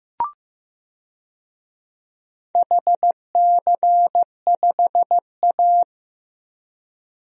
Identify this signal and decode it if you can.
Morse code, heard as a keyed tone, about 15 wpm, 700 Hz. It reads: HC5A
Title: DXCC-100 Call Signs - 4 Characters - 15wpm